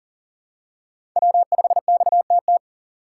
Morse code, heard as a keyed tone, about 37 wpm, 700 Hz. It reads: W5XTT
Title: USA Call Signs - 5 Characters - 6x Character Spacing - 40wpm